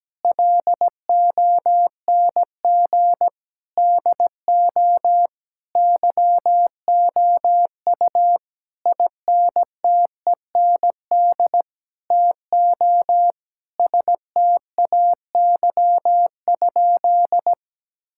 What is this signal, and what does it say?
Morse code, heard as a keyed tone, about 17 wpm, 700 Hz. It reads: LONG DO YOU INTEND TO STAY?
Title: Sentences from Top 1600 Words 17wpm